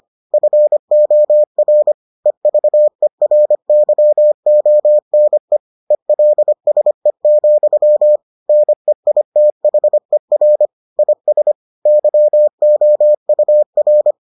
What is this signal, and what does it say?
Morse code, heard as a keyed tone, about 25 wpm, 600 Hz. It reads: FOR EVERYONE ELSE, NEITHER IS YOUR